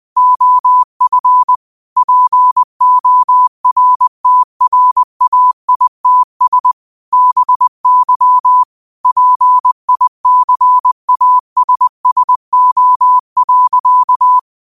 Morse code, heard as a keyed tone, about 20 wpm, 1000 Hz. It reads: OF PORTRAITS BY PICASSO.